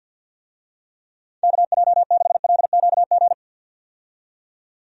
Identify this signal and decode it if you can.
Morse code, heard as a keyed tone, about 52 wpm, 700 Hz. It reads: KJ6LQG
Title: USA Call Signs - 6 Characters - 50wpm